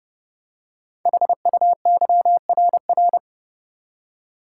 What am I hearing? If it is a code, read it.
Morse code, heard as a keyed tone, about 30 wpm, 700 Hz. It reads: HUYRR
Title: Code Groups - 5 Letters - 30wpm